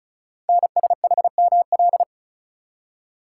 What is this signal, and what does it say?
Morse code, heard as a keyed tone, about 36 wpm, 690 Hz. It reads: NSHML
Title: Code Groups - 5 Letters - 35wpm